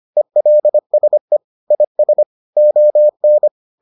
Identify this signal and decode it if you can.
Morse code, heard as a keyed tone, about 25 wpm, 600 Hz. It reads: ELSE IS ON